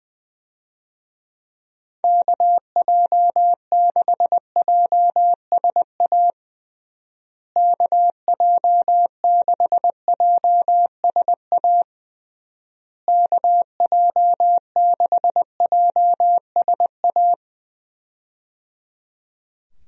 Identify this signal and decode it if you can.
Morse code, heard as a keyed tone, about 20 wpm, 700 Hz. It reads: KJ6JSA KJ6JSA KJ6JSA
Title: USA Call Signs - 6 Characters - Repeated 3x - 20wpm